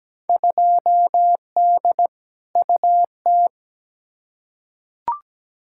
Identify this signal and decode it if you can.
Morse code, heard as a keyed tone, about 17 wpm, 700 Hz. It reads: 2D UT
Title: Field Day Exchanges Encore - 17wpm